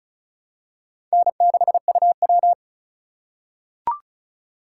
Morse code, heard as a keyed tone, about 35 wpm, 700 Hz. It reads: N6UW
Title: USA Call Signs - 4 Characters - 2x Character Spacing - 35wpm